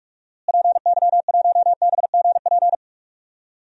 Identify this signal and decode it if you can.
Morse code, heard as a keyed tone, about 41 wpm, 700 Hz. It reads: PY1BGP
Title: DXCC-100 Call Signs - 6 Characters - Repeated 3x - 45wpm